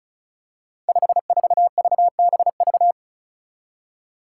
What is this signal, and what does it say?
Morse code, heard as a keyed tone, about 36 wpm, 700 Hz. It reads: 54VBV
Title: Five Challenging Letter-Number Combinations 35wpm